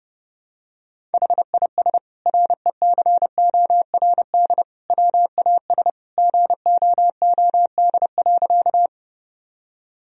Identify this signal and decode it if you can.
Morse code, heard as a keyed tone, about 32 wpm, 700 Hz. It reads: HIS RECORD WAS GOOD.
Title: Sentences from Top 1000 Words - Review Part 3 - 30wpm